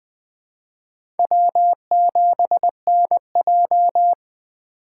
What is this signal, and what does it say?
Morse code, heard as a keyed tone, about 20 wpm, 700 Hz. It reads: W7NJ
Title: USA Call Signs - 3x Character Spacing - 20wpm